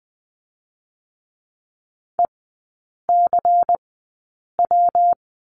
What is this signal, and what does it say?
Morse code, heard as a keyed tone, about 20 wpm, spaced as for 8 wpm, 700 Hz. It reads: ECW
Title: 121 Lesson 20e - Number 3 - Three Characters 20wpm